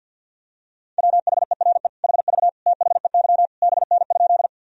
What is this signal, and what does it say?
Morse code, heard as a keyed tone, about 51 wpm, 700 Hz. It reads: WHERE HV THEY BN?
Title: Sentences from Top 200 Words and Abbreviations 50wpm